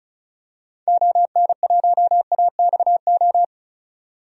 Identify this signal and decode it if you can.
Morse code, heard as a keyed tone, about 36 wpm, 700 Hz. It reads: ON1AXO
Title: DXCC-100 Call Signs - 6 Characters - 35wpm